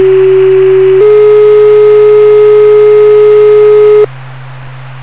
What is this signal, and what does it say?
Two-tone page, heard as a one-second tone, then a three-second tone, about 380 then 420 Hz. It is heard from 0.0 to 4.1 s.
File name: two-tone-paging-378hz-422hz-1sec-3sec